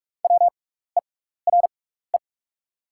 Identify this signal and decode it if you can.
Morse code, heard as a keyed tone, about 44 wpm, spaced as for 15 wpm, 690 Hz. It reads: WERE
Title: 6x Character Spacing - Top 200 Words - 45wpm